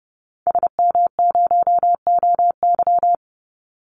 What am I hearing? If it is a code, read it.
Morse code, heard as a keyed tone, about 30 wpm, 700 Hz. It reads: SM0OY
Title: Call Signs - 30wpm